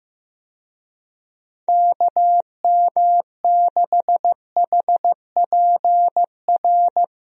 Morse code, heard as a keyed tone, about 15 wpm, 700 Hz. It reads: KM6HPR
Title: USA Call Signs - 6 Characters - Repeated 3x - 15wpm